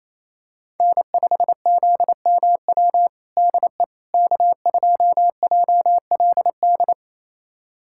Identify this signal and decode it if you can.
Morse code, heard as a keyed tone, about 28 wpm, 700 Hz. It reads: N5ZMW DE K2JLD
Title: USA Callsign DE Callsign - 5 Characters - 28wpm